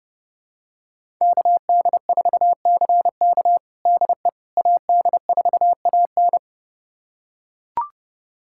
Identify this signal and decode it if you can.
Morse code, heard as a keyed tone, about 30 wpm, 700 Hz. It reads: KD4CK DE AD4AN
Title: USA Callsign DE Callsign - 5 Characters - 30wpm